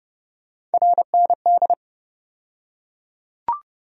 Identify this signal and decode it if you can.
Morse code, heard as a keyed tone, about 30 wpm, 700 Hz. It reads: RND
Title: USA Call Sign Postfixes - 3 Letters 30wpm